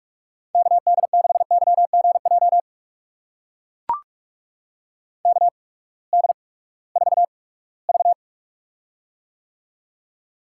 Morse code, heard as a keyed tone, about 44 wpm, spaced as for 22 wpm, 700 Hz. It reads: KD6YGJ KD4V
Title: USA Call Signs - 8x Character Spacing - 45wpm